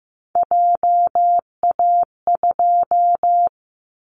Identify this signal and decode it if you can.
Morse code, heard as a keyed tone, about 15 wpm, 700 Hz. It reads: JA2
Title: Callsign Prefixes for Top 50 DXCC - 15wpm